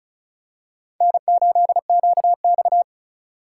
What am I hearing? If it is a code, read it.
Morse code, heard as a keyed tone, about 36 wpm, 700 Hz. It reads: N8QX
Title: USA Call Signs - 4 Characters - 4x Character Spacing - 35wpm